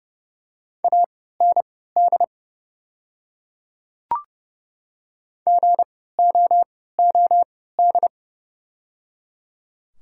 Morse code, heard as a keyed tone, about 30 wpm, spaced as for 17 wpm, 690 Hz. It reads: AND GOOD
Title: 3x Character Spacing - Top 300 Words - 30wpm